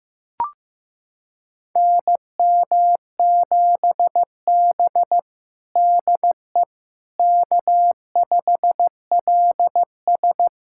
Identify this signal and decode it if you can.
Morse code, heard as a keyed tone, about 15 wpm, 700 Hz. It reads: NM7B DE K5LS
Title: USA Callsign DE Callsign - 4 Characters - 15wpm